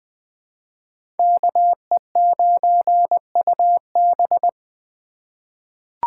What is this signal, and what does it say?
Morse code, heard as a keyed tone, about 20 wpm, 700 Hz. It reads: KE9UB
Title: USA Call Signs - 5 Characters - 4x Character Spacing - 20wpm